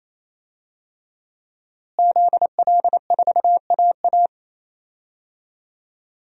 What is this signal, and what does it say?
Morse code, heard as a keyed tone, about 28 wpm, 700 Hz. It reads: ZL4AA
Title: DXCC-100 Call Signs - 5 Characters - Repeated 3x - 28wpm